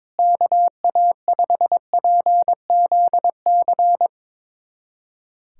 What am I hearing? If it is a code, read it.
Morse code, heard as a keyed tone, about 22 wpm, 700 Hz. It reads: KA5PZC
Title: USA Callsign DE Callsign - 6 Characters - 22wpm